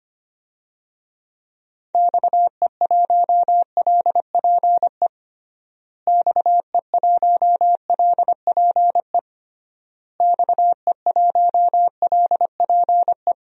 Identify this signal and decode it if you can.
Morse code, heard as a keyed tone, about 25 wpm, 700 Hz. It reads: XE1LPE XE1LPE XE1LPE
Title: DXCC-100 Call Signs - 6 Characters - Repeated 3x - 25wpm